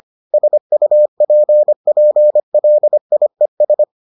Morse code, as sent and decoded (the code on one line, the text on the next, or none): ... ..- .--. .--. .-.. .. ...
SUPPLIES